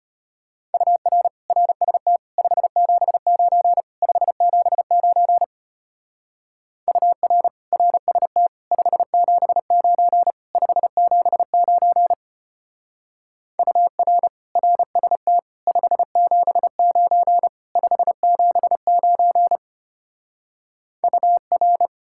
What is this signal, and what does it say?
Morse code, heard as a keyed tone, about 31 wpm, 700 Hz. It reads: UR RST 579 579 UR RST 579 579 UR RST 579 579 UR
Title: Speed-Racing Common QSO Phrases - 25wpm